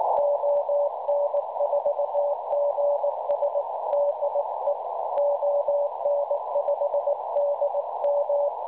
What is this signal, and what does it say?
Morse code, heard as a keyed tone, about 18 wpm, 570 Hz, with noise. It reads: ON4GS DE ON5DM
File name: cwronde / websdr recording .6kHz